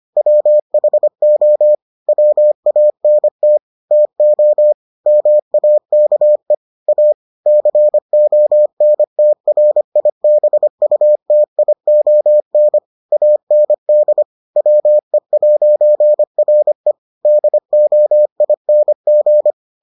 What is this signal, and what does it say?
Morse code, heard as a keyed tone, about 25 wpm, 600 Hz. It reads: WHO WANT TO MAKE A CONTRIBUTION AND WE'RE DOING